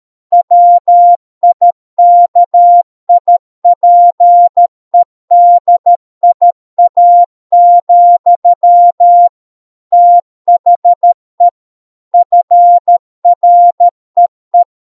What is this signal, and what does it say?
Morse code, heard as a keyed tone, about 13 wpm, 700 Hz. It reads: WIKIPEDIA, THE FREE